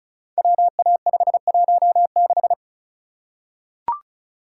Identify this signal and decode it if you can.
Morse code, heard as a keyed tone, about 36 wpm, 700 Hz. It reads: WA516
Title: Code Groups - 5 Letter:Numbers - 35wpm